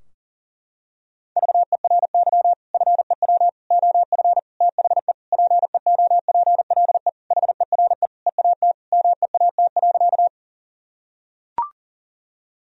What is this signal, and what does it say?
Morse code, heard as a keyed tone, about 40 wpm, 700 Hz. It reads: VERY FEW OF THE PEOPLE HERE EAT MEAT.
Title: Sentences from Top 1000 Words - Review Part 3 - 40wpm